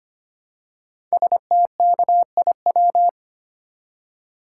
Morse code, heard as a keyed tone, about 25 wpm, 700 Hz. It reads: STKIW
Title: Code Groups - 5 Letters - 25wpm